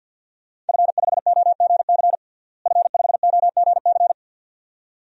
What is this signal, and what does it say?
Morse code, heard as a keyed tone, about 51 wpm, 700 Hz. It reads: F5OGC F5OGC
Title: DXCC-100 Call Signs - 5 Characters - Repeated 2x - 50wpm